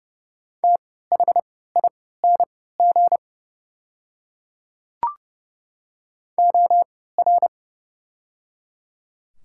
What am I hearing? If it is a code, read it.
Morse code, heard as a keyed tone, about 30 wpm, spaced as for 17 wpm, 700 Hz. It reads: THING OR
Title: 3x Character Spacing - Top 100 Words - 30wpm